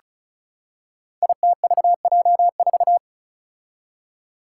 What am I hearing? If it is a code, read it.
Morse code, heard as a keyed tone, about 36 wpm, 700 Hz. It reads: ITVJ4
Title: Five Challenging Letter-Number Combinations 35wpm